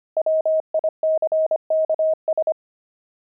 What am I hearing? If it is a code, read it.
Morse code, heard as a keyed tone, about 25 wpm, 630 Hz. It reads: WICKS
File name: pangram-fs-25-cs-25-nine